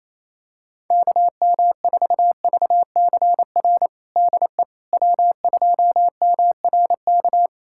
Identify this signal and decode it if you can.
Morse code, heard as a keyed tone, about 28 wpm, 700 Hz. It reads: KM4VCR DE W2MRK
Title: USA Callsign DE Callsign 28wpm